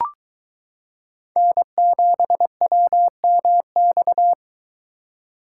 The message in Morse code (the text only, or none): N7WMX